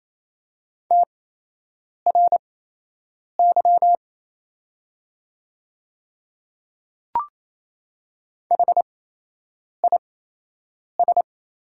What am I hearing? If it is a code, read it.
Morse code, heard as a keyed tone, about 27 wpm, spaced as for 8 wpm, 700 Hz. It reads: TRY HIS
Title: 8x Character Spacing - Top 200 Words - 28wpm